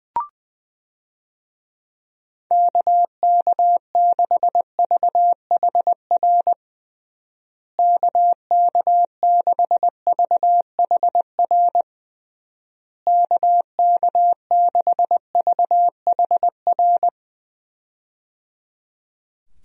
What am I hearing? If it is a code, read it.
Morse code, heard as a keyed tone, about 20 wpm, 700 Hz. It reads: KK6VHR KK6VHR KK6VHR
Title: USA Call Signs - 6 Characters - Repeated 3x - 20wpm